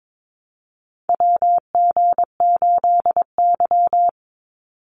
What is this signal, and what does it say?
Morse code, heard as a keyed tone, about 22 wpm, 700 Hz. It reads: WG8Y
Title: NA SOTA Chaser Callsigns 22wpm